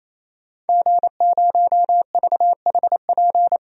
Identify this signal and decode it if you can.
Morse code, heard as a keyed tone, about 28 wpm, 700 Hz. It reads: G0VHP
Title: DXCC-100 Call Signs - 5 Characters - 28wpm